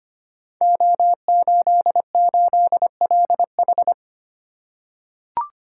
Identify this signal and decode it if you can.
Morse code, heard as a keyed tone, about 25 wpm, 700 Hz. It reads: O88LH
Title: Code Groups - 5 Letter:Numbers - 25wpm